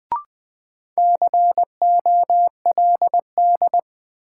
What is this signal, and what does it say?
Morse code, heard as a keyed tone, about 20 wpm, 700 Hz. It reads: COLD
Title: Sets of 5 Words - New Word Drill - 20wpm